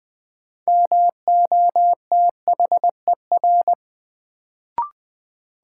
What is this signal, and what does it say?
Morse code, heard as a keyed tone, about 20 wpm, 700 Hz. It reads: MOTHER
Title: Sentences from Top 200 Words - New Word Drill - 20wpm